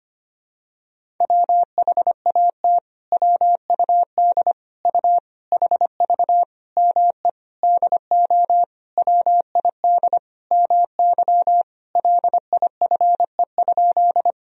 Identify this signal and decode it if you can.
Morse code, heard as a keyed tone, about 25 wpm, 700 Hz. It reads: WHAT WUD U HV ME DO WID MY LIFE?
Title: Sentences from Top 300 Words and Abbreviations 25wpm